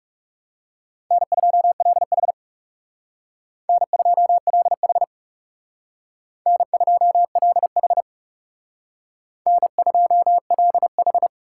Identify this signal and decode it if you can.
Morse code, heard as a keyed tone, about 40 wpm, 700 Hz. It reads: N2LH N2LH N2LH N2LH